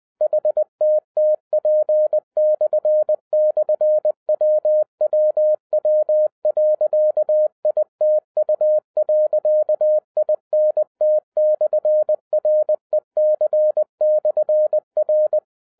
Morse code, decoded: HTTP//WWW.ITU.INT/REC/R

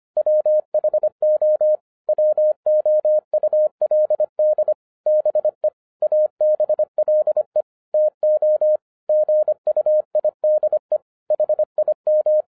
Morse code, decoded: WHO WOULD BE ABLE TO GUIDE HIM

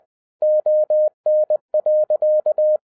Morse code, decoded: ON.

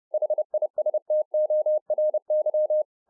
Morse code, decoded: HISTORY

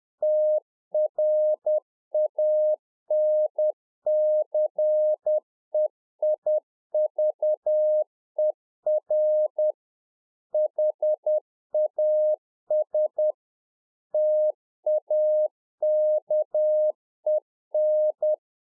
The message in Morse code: TRANCEIVER HAS TAKEN

